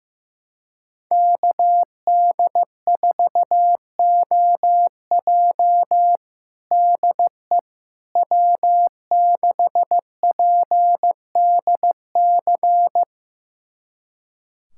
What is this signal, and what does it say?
Morse code, heard as a keyed tone, about 15 wpm, 700 Hz. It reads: KD4OJ DE W6PDC
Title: USA Callsign DE Callsign - 5 Characters - 15wpm